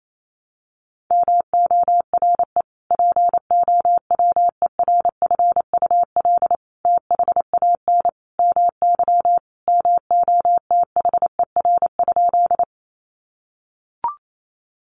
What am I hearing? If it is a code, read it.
Morse code, heard as a keyed tone, about 28 wpm, 700 Hz. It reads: MORE POWERFUL THAN MY MOTHER?
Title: Sentences from Top 1300 Words 28wpm